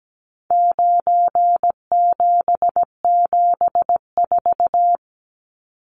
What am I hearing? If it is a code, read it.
Morse code, heard as a keyed tone, about 17 wpm, 700 Hz. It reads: 9774
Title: Numbers - 4-Digit 17wpm